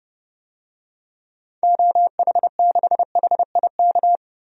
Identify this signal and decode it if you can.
Morse code, heard as a keyed tone, about 30 wpm, 700 Hz. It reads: OH6HIK
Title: DXCC-100 Call Signs - 6 Characters - 30wpm